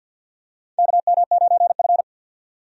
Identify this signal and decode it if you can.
Morse code, heard as a keyed tone, about 53 wpm, 700 Hz. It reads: KM9F
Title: USA Call Signs - 4 Characters - 50wpm